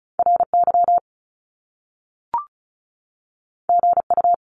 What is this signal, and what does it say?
Morse code, heard as a keyed tone, about 36 wpm, 700 Hz. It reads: RY GU
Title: Top Two Letter Combinations in Words 35wpm